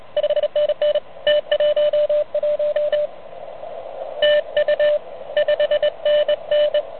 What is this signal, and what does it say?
Morse code, heard as a keyed tone, about 30 wpm, 590 Hz. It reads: 5NN T11 TU 5NN
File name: truSDX / MWC29.8